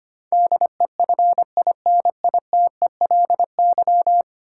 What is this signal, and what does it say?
Morse code, heard as a keyed tone, about 25 wpm, 700 Hz. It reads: DEFINITELY
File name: Ten Letter Words - 25wpm